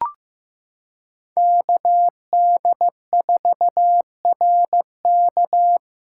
Morse code, heard as a keyed tone, about 15 wpm, 700 Hz. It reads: KD4RK
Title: USA Callsign DE Callsign 15wpm